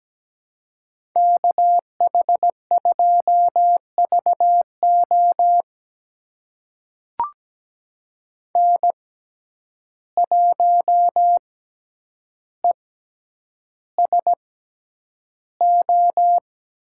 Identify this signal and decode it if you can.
Morse code, heard as a keyed tone, about 17 wpm, spaced as for 9 wpm, 700 Hz. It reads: KH2VO N1ESO
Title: USA Call Signs - 5 Characters - 6x Character Spacing - 17wpm